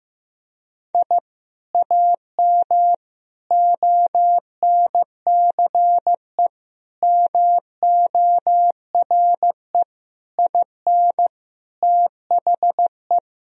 Morse code, heard as a keyed tone, about 16 wpm, 700 Hz. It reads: I AM ONCE MORE IN THE
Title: Sentences from Top 1000 Words - Review Part 3 - 15wpm